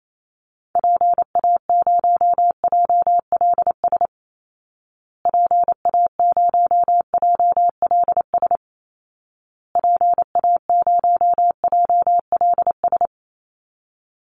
Morse code, heard as a keyed tone, about 28 wpm, 700 Hz. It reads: PA0JLS PA0JLS PA0JLS
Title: Call Signs Repeated 2x - 28wpm